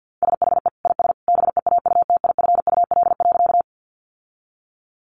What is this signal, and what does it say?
Morse code, heard as a keyed tone, about 50 wpm, 700 Hz. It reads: SHE IS BEAUTIFUL.
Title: Sentences from Top 900 Words 50wpm